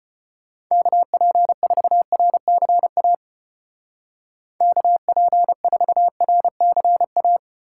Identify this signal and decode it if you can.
Morse code, heard as a keyed tone, about 31 wpm, 700 Hz. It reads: KP4RCA KP4RCA
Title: Speed Racing - USA 2x3 Call Signs 25wpm